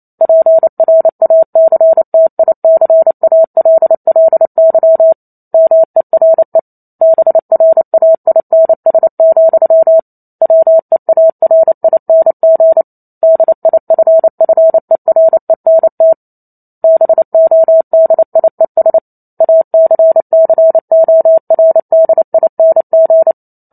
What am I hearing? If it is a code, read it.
Morse code, heard as a keyed tone, about 28 wpm, 650 Hz. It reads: PRACTICALLY MERE BRAINS, WEARING DIFFERENT BODIES ACCORDING
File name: Chapter 2 - 28 WPM